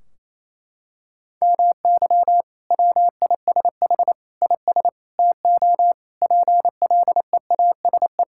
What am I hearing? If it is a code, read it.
Morse code, heard as a keyed tone, about 28 wpm, 700 Hz. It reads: MY WISH IS TO PLEASE
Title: Sentences from Top 1600 Words 28wpm